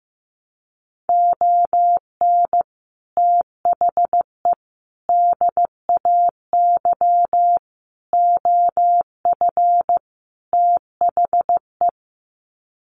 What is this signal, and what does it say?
Morse code, heard as a keyed tone, about 15 wpm, 700 Hz. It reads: ON THE DAY OF THE
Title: Sets of 5 Words Encore 15wpm